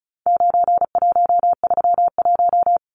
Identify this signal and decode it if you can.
Morse code, heard as a keyed tone, about 34 wpm, 700 Hz. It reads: 9131
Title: Numbers - 4-Digit 35wpm